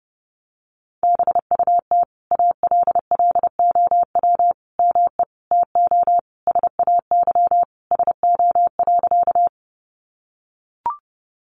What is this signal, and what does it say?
Morse code, heard as a keyed tone, about 30 wpm, 700 Hz. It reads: BUT ALLOW ME TO SAY SO.
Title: Sentences from Top 700 Words 30wpm